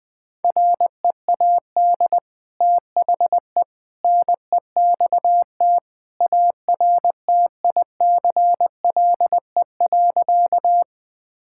Morse code, heard as a keyed tone, about 20 wpm, 700 Hz. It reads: READ THE NEXT ARTICLE.
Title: Sentences from Top 1000 Words - Review Part 5 - 20wpm